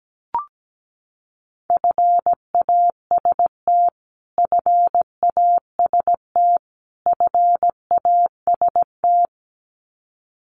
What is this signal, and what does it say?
Morse code, heard as a keyed tone, about 17 wpm, 700 Hz. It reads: FAST FAST FAST
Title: Top 500 Words Repeated 17wpm